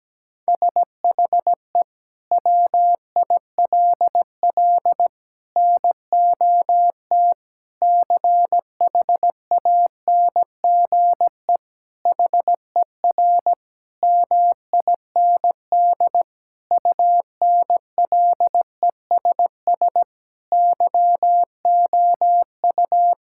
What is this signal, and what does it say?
Morse code, heard as a keyed tone, about 17 wpm, 700 Hz. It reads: SHE WILL NOT CHANGE HER MIND UNLESS YOU
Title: Sentences from Top 1000 Words - Review Part 6 - 17wpm